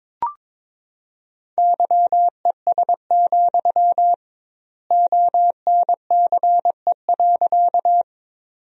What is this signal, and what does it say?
Morse code, heard as a keyed tone, about 22 wpm, 700 Hz. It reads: YES, ONCE.
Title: 2x Word-Spacing - Sentences from Top 500 Words - 22wpm